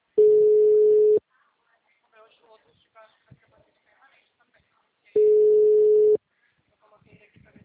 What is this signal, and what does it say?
Ringback tone (Europe, 425 Hz), with speech.